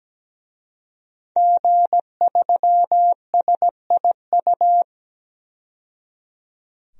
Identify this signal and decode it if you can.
Morse code, heard as a keyed tone, about 17 wpm, 700 Hz. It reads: G3SIU
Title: DXCC-100 Call Signs - 5 Characters - 17wpm